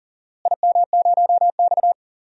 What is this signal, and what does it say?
Morse code, heard as a keyed tone, about 40 wpm, 700 Hz. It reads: IM0X
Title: DXCC-100 Call Signs - 4 Characters - Repeated 3x - 40wpm